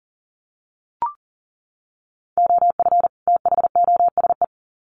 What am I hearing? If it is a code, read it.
Morse code, heard as a keyed tone, about 40 wpm, 700 Hz. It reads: OF THOSE
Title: Sets of 2 Words 40wpm